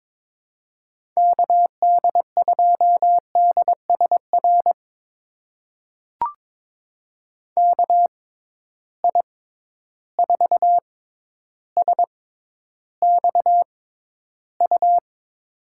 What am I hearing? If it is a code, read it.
Morse code, heard as a keyed tone, about 22 wpm, spaced as for 11 wpm, 700 Hz. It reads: KD2DSR KI4SXU
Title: USA Call Signs - 6x Character Spacing - 22wpm